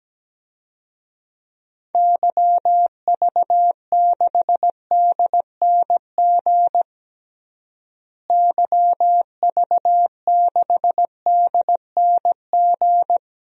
Morse code, heard as a keyed tone, about 17 wpm, 700 Hz. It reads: YV6DNG YV6DNG
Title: DXCC-100 Call Signs - 6 Characters - Repeated 2x - 17wpm